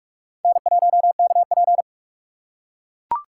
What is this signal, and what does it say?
Morse code, heard as a keyed tone, about 43 wpm, 700 Hz. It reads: N1KP